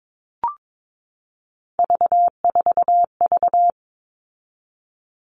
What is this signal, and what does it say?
Morse code, heard as a keyed tone, about 22 wpm, 700 Hz. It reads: V4V
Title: All Cut Numbers - 3-Digit 22wpm